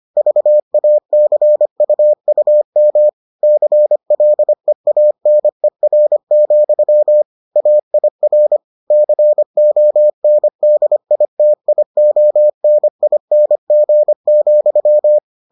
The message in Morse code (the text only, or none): VACUUM CLEANER, AIR CONDITIONING,